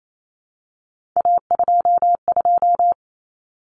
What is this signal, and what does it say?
Morse code, heard as a keyed tone, about 28 wpm, 700 Hz. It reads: A22
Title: Common Cut Numbers - 3-Digit 28wpm